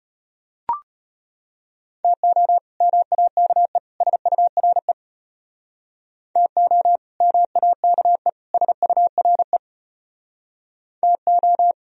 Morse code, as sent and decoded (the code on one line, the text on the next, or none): - --- / -- .- -.- . / ... ..- .-. / - --- / -- .- -.- . / ... ..- .-. / - ---
TO MAKE SURE TO MAKE SURE TO